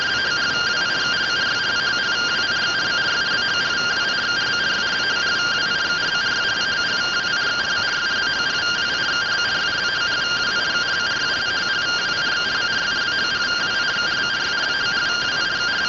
Two-tone teleprinter signal, about 170 Hz shift.